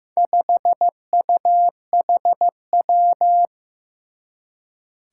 Morse code, heard as a keyed tone, about 15 wpm, 700 Hz. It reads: 5UHW
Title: Four Challenging Letter-Number Combinations 15wpm